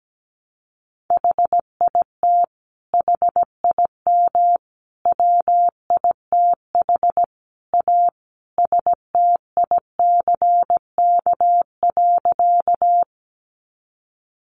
Morse code, heard as a keyed tone, about 17 wpm, 700 Hz. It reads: HIT HIM WITH A STICK.
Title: Sentences from Top 1200 Words 17wpm